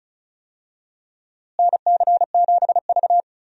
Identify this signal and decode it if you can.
Morse code, heard as a keyed tone, about 36 wpm, 700 Hz. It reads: NC7V
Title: USA Call Signs - 4 Characters - Repeated 2x - 35wpm